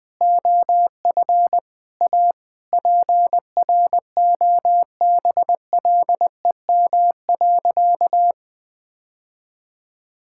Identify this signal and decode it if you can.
Morse code, heard as a keyed tone, about 20 wpm, 700 Hz. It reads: OF A PROBLEM.
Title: Sentences with 8-Word Phrases - 20wpm